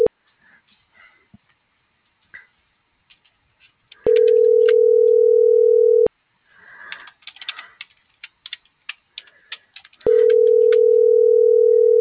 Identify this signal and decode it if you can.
Ringback tone (North America), 440 + 480 Hz.
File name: CENTRAL GENERAL / out eduardo.alatriste